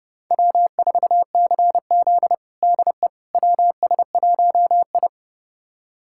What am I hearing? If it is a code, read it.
Morse code, heard as a keyed tone, about 30 wpm, 700 Hz. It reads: W4CZ DE WS1I
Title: USA Callsign DE Callsign - 4 Characters - 30wpm